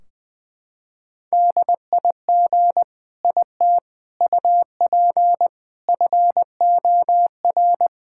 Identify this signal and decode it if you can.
Morse code, heard as a keyed tone, about 20 wpm, 700 Hz. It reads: DIG IT UP FOR
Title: Sentences from Top 1600 Words 20wpm